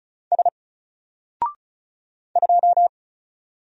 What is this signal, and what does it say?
Morse code, heard as a keyed tone, about 34 wpm, 700 Hz. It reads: S 2
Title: Single Letter-Number Alt 35wpm